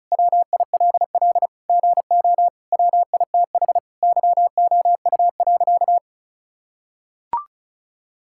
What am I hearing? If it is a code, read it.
Morse code, heard as a keyed tone, about 36 wpm, 700 Hz. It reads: WILL GO WITH YOU.
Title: Sentences from Top 1600 Words 35wpm